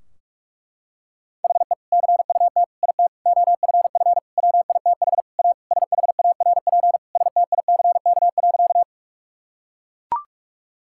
Morse code, heard as a keyed tone, about 42 wpm, 700 Hz. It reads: HE CUT IT OFF WITH A SHARP STICK.
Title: Sentences from Top 1600 Words 45wpm